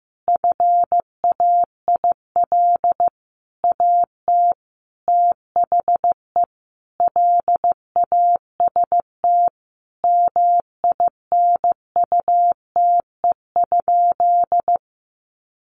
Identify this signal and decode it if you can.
Morse code, heard as a keyed tone, about 15 wpm, 700 Hz. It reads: FAIL AT THE LAST MINUTE?
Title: Sentences from Top 1100 Words 15wpm